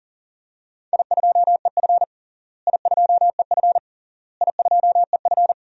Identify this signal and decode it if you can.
Morse code, heard as a keyed tone, about 40 wpm, 700 Hz. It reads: I2EF I2EF I2EF
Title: DXCC-100 Call Signs - 4 Characters - Repeated 3x - 40wpm